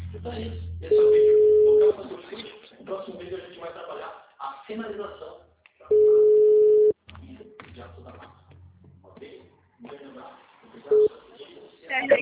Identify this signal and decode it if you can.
Ringback tone (Europe, 425 Hz), with speech.